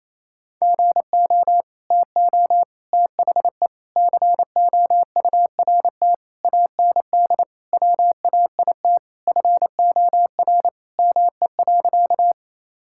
Morse code, heard as a keyed tone, about 28 wpm, 700 Hz. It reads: GO TO THE COURT AND WAIT FOR ME.
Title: Sentences from Top 1600 Words 28wpm